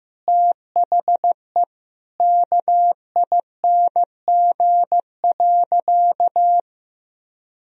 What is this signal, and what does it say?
Morse code, heard as a keyed tone, about 15 wpm, 700 Hz. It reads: THE KING.
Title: Sentences from Top 400 Words and Abbreviations 15wpm